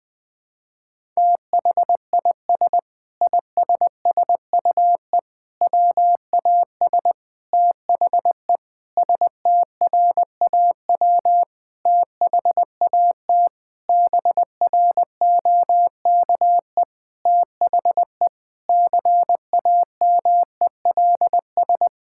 Morse code, heard as a keyed tone, about 21 wpm, 700 Hz. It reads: THIS ISSUE WAS THE STRAW THAT BROKE THE CAMELS
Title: Sentences with 7-Word Phrases - Spoken Beginning - 20wpm